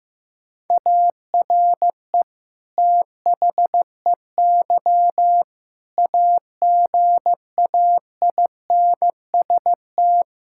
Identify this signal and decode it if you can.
Morse code, heard as a keyed tone, about 15 wpm, 700 Hz. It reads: ARE THEY AGAINST
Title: Sentences from Top 1000 Words - Review Part 5 - 15wpm